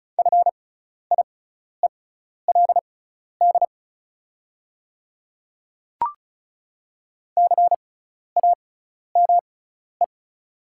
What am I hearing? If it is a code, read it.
Morse code, heard as a keyed tone, about 34 wpm, spaced as for 12 wpm, 690 Hz. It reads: FIELD CAME
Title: 6x Character Spacing - Top 500 Words - 35wpm